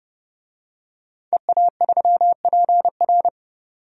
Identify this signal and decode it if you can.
Morse code, heard as a keyed tone, about 30 wpm, 690 Hz. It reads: EA3PR
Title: DXCC-100 Call Signs - 5 Characters - 30wpm